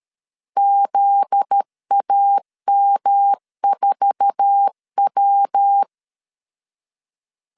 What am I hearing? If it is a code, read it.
Morse code, heard as a keyed tone, about 13 wpm, 800 Hz. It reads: ZAM4W
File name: CW / DET / 1 9 1500C S C 12wpm 3811